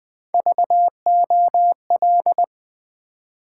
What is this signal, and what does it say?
Morse code, heard as a keyed tone, about 20 wpm, 700 Hz. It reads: VOL
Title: Build A Word - 3 Letters At A Time - 9-Letter Words - 20wpm